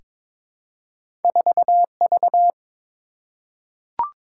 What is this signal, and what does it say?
Morse code, heard as a keyed tone, about 22 wpm, 700 Hz. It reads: 4V
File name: practice / 22wpm / Two Letter-Number Combinations - 22wpm